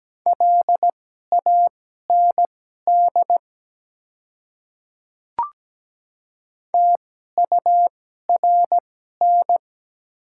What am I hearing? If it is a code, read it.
Morse code, heard as a keyed tone, about 17 wpm, spaced as for 12 wpm, 700 Hz. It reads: LAND TURN
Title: 2x Character Spacing - Top 200 Words - 17wpm